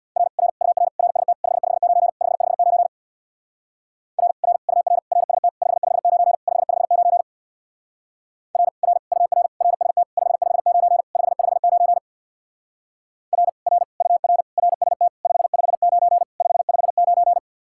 Morse code, heard as a keyed tone, about 60 wpm, 700 Hz. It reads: R R UR RST 559 559 R R UR RST 559 559 R R UR RST 559 559 R R UR RST 559 559
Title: Speed-Racing Common QSO Phrases Encore - 50wpm